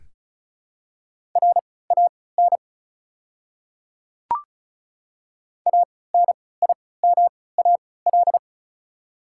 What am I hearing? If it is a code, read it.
Morse code, heard as a keyed tone, about 34 wpm, spaced as for 20 wpm, 700 Hz. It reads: RAN ANIMAL
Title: 3x Character Spacing - Top 500 Words - 35wpm